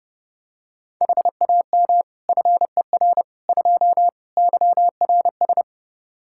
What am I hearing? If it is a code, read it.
Morse code, heard as a keyed tone, about 30 wpm, 700 Hz. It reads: HAM FER 2 YRS
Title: Practice QSO - Exchange 3 - More Info - 30wpm